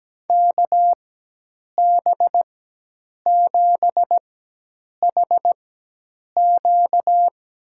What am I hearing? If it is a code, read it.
Morse code, heard as a keyed tone, about 17 wpm, spaced as for 8 wpm, 700 Hz. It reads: KB7HQ
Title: USA Call Signs - 4x Character Spacing - 17wpm